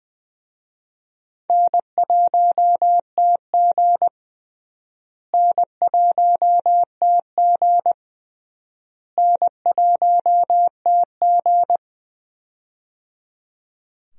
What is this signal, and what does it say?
Morse code, heard as a keyed tone, about 19 wpm, 700 Hz. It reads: N1TG N1TG N1TG
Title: USA Call Signs - 4 Characters - Repeated 3x - 20wpm